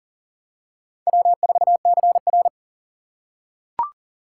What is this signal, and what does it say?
Morse code, heard as a keyed tone, about 40 wpm, 700 Hz. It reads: W4CR
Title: USA Call Signs - 6x Character Spacing - 40wpm